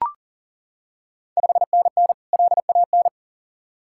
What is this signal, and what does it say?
Morse code, heard as a keyed tone, about 37 wpm, 690 Hz. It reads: 5NN LAN
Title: Ontario QSO Party Exchange 40wpm